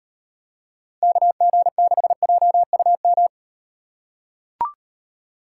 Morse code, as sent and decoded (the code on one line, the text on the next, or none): -.- --. -.... .--- ..- --
KG6JUM